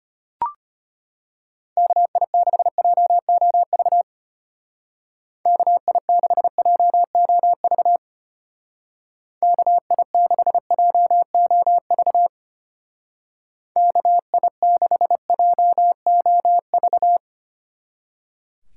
Morse code, heard as a keyed tone, about 32 wpm, 700 Hz. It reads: KI6JOV KI6JOV KI6JOV KI6JOV